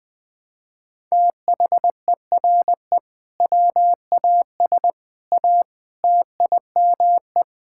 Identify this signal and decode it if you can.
Morse code, heard as a keyed tone, about 20 wpm, 700 Hz. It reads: THERE WAS A TIME
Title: Sentences with 5-Word Phrases - Spoken Beginning - 20wpm